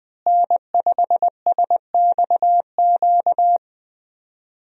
Morse code, heard as a keyed tone, about 21 wpm, 700 Hz. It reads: N5SXQ
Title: USA Call Signs - 5 Characters - Repeated 2x - 20wpm